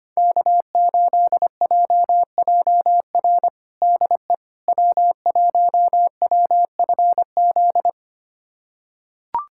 Morse code, heard as a keyed tone, about 25 wpm, 700 Hz. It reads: K8JJR DE W1WFZ
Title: USA Callsign DE Callsign - 5 Characters - 25wpm